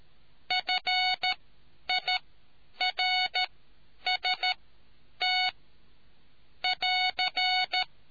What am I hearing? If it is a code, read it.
Morse code, heard as a keyed tone, about 13 wpm, spaced as for 9 wpm, 2200 Hz. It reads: FIRST <AR>